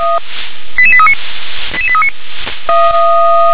Five-tone selective call (ZVEI), heard as quick runs of five tones, tones of about 70 ms. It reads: 88839 88839
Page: /357/SDR/ZVEI/88743/